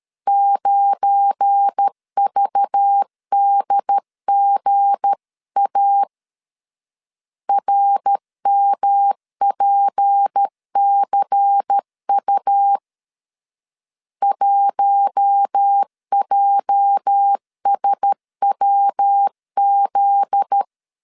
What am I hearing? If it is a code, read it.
Morse code, heard as a keyed tone, about 13 wpm, 800 Hz. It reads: 9VDGA RMPCU 1JSWZ
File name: CW / DET / 1 9 1500C S A 12wpm 3800